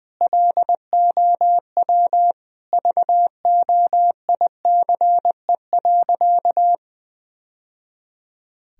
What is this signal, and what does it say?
Morse code, heard as a keyed tone, about 20 wpm, 700 Hz. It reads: LOW VOICE.
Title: Sentences from Top 1000 Words - Review Part 1 - 20wpm